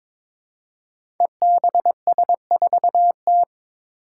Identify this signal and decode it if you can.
Morse code, heard as a keyed tone, about 22 wpm, 700 Hz. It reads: EBS4T
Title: Five Challenging Letter-Number Combinations 22wpm